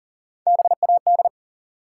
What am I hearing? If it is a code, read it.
Morse code, heard as a keyed tone, about 40 wpm, 700 Hz. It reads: BAD
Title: Three Letter Words - 40wpm